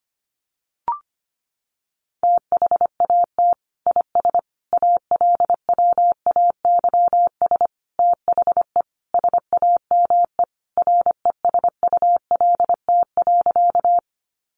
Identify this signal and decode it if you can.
Morse code, heard as a keyed tone, about 25 wpm, 700 Hz. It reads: THAT IS ALWAYS THE SAME RESULT.
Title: Sentences from Top 700 Words 25wpm